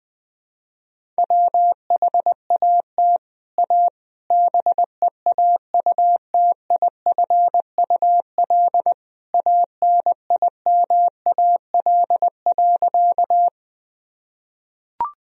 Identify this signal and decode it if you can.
Morse code, heard as a keyed tone, about 20 wpm, 700 Hz. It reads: WHAT A BEAUTIFUL ANIMAL.
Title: Sentences from Top 1000 Words - Review Part 1 - 20wpm